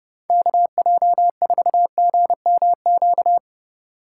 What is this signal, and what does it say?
Morse code, heard as a keyed tone, about 30 wpm, 700 Hz. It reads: KJ4GMQ